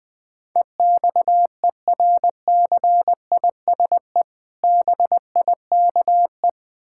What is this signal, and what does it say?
Morse code, heard as a keyed tone, about 20 wpm, 700 Hz. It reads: EXERCISE BIKE
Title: Easy Words - Speed-Racing with Hints - 20wpm